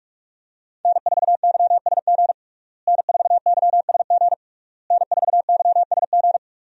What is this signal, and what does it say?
Morse code, heard as a keyed tone, about 41 wpm, 700 Hz. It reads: N4YSG N4YSG N4YSG
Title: USA Call Signs - 5 Characters - Repeated 3x - 45wpm